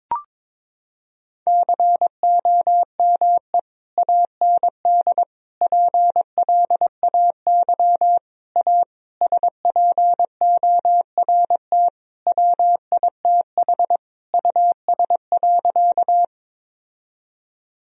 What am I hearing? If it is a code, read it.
Morse code, heard as a keyed tone, about 22 wpm, 700 Hz. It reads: COME AND PLAY A SPORT WITH US.
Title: Sentences from Top 1000 Words - Review Part 4 - 22wpm